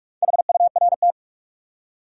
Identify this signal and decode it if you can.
Morse code, heard as a keyed tone, about 43 wpm, 700 Hz. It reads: HURT
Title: Four Letter Words - 45wpm